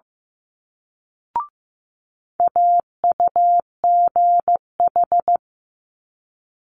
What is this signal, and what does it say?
Morse code, heard as a keyed tone, about 15 wpm, 700 Hz. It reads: AUGH
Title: Multi-letter Phonemes Quick Fire 15wpm